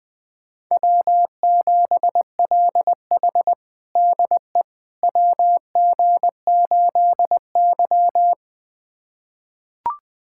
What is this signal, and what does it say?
Morse code, heard as a keyed tone, about 21 wpm, 700 Hz. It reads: W7LH DE WG8Y
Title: USA Callsign DE Callsign - 4 Characters - 20wpm